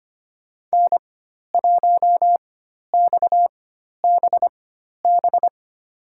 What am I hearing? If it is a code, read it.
Morse code, heard as a keyed tone, about 25 wpm, spaced as for 12 wpm, 700 Hz. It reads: N1XBB
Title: USA Call Signs - 5 Characters - 4x Character Spacing - 25wpm